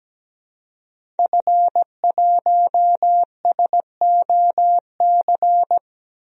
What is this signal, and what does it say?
Morse code, heard as a keyed tone, about 17 wpm, 700 Hz. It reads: F1SOC
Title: DXCC-100 Call Signs - 5 Characters - Repeated 2x - 17wpm